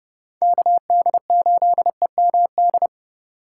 Morse code, heard as a keyed tone, about 30 wpm, 700 Hz. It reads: KD8EMD